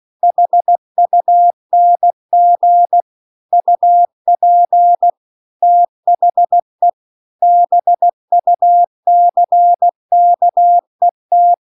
Morse code, heard as a keyed tone, about 16 wpm, 700 Hz. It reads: HUNG UP THE BUCKET